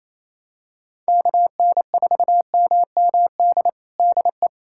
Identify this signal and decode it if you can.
Morse code, heard as a keyed tone, about 27 wpm, 700 Hz. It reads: KN4MMD DE
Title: USA Callsign DE Callsign - 6 Characters - 28wpm